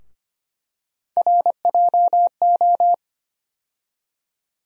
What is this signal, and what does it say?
Morse code, heard as a keyed tone, about 25 wpm, 700 Hz. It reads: RJO
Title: USA Call Sign Postfixes - 3 Letters 25wpm